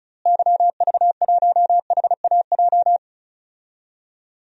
Morse code, heard as a keyed tone, about 36 wpm, 700 Hz. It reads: YV1HAJ
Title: DXCC-100 Call Signs - 6 Characters - Repeated 2x - 35wpm